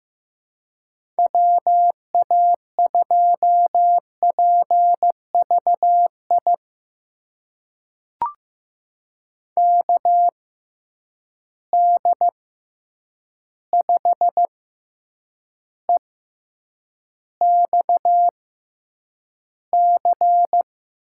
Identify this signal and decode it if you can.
Morse code, heard as a keyed tone, about 15 wpm, spaced as for 8 wpm, 700 Hz. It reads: WA2PVI KD5EXC
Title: USA Call Signs - 6 Characters - 6x Character Spacing - 15wpm